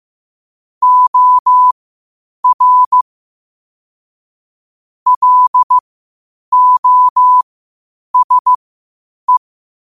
Morse code, heard as a keyed tone, about 15 wpm, spaced as for 8 wpm, 1000 Hz. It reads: OR LOSE
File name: 8wpm